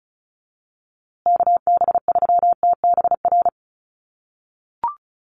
Morse code, heard as a keyed tone, about 34 wpm, 700 Hz. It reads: KB3TBR
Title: USA Call Signs Repeated 3x 35wpm